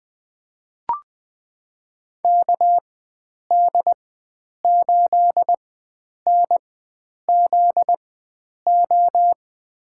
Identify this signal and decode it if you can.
Morse code, heard as a keyed tone, about 20 wpm, spaced as for 9 wpm, 700 Hz. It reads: KD8NZO
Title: USA Call Signs - 6 Characters - 4x Character Spacing - 20wpm